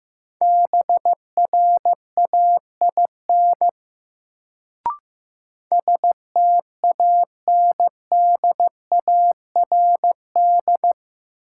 Morse code, heard as a keyed tone, about 15 wpm, 700 Hz. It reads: BRAIN STANDARD
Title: Sentences from Top 1000 Words - Review Part 6 - New Word Drill - 15wpm